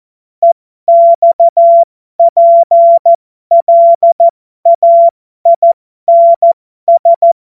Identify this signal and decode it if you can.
Morse code, heard as a keyed tone, about 14 wpm, spaced as for 12 wpm, 680 Hz. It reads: EXPLAINS
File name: Morsekurs / Witze